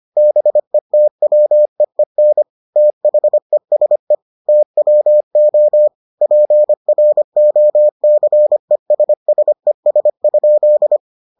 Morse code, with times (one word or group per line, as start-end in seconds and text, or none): 0.2-2.4 BETWEEN
2.8-4.1 THESE
4.5-5.9 TWO
6.2-11.0 PROCESSES?